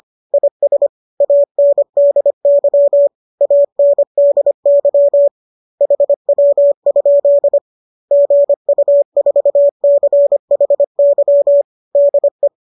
Morse code, heard as a keyed tone, about 25 wpm, 570 Hz. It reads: IS ANDY ANDY HW? GU4CHY DE
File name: qso210 25